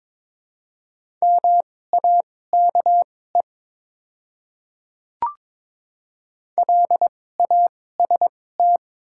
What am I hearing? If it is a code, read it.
Morse code, heard as a keyed tone, about 21 wpm, spaced as for 16 wpm, 700 Hz. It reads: MAKE LAST
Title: 2x Character Spacing - Top 300 Words - 22wpm